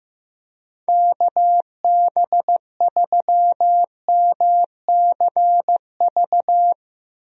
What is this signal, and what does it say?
Morse code, heard as a keyed tone, about 15 wpm, 700 Hz. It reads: KB3MCV
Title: USA Call Signs - 6 Characters - Repeated 2x - 15wpm